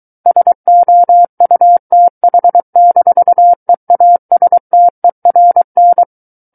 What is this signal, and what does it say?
Morse code, heard as a keyed tone, about 23 wpm, 700 Hz. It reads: SOUTH-EASTERN
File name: Book One / 14-Chapter 14 - 23 WPM